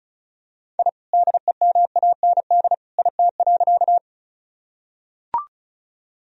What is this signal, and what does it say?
Morse code, heard as a keyed tone, about 34 wpm, 700 Hz. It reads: I DEMAND IT.
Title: Sentences from Top 1000 Words - Review Part 4 - 35wpm